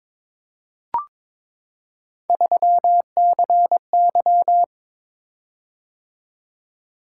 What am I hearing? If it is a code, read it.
Morse code, heard as a keyed tone, about 22 wpm, 700 Hz. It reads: 3CY
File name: practice / 22wpm / Three Letter-Number Combinations - 22wpm